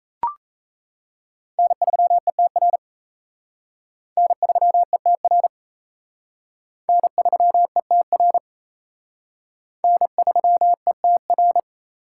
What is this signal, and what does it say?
Morse code, heard as a keyed tone, about 34 wpm, 700 Hz. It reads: N3ETR N3ETR N3ETR N3ETR
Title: Speed Racing - USA 1x3 Call Signs 28wpm